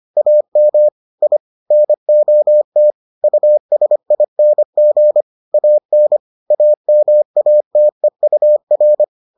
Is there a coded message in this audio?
Yes: Morse code from 0.2 to 9.0 s, heard as a keyed tone, about 25 wpm, 600 Hz. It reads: AM I NOT USING AN AMATEUR